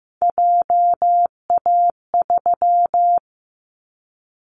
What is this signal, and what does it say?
Morse code, heard as a keyed tone, about 15 wpm, 700 Hz. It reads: JA3
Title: Callsign Prefixes for Top 25 DXCC - 15wpm